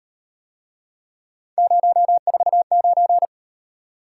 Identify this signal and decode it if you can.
Morse code, heard as a keyed tone, about 37 wpm, 700 Hz. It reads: 049